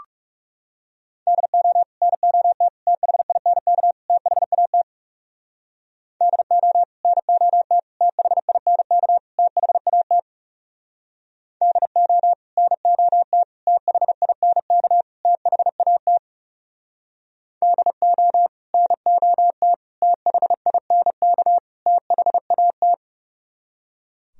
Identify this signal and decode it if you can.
Morse code, heard as a keyed tone, about 36 wpm, 700 Hz. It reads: DO NOT THINK THAT DO NOT THINK THAT DO NOT THINK THAT DO NOT THINK THAT
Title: Speed-Racing - Sets of 4 Words 30wpm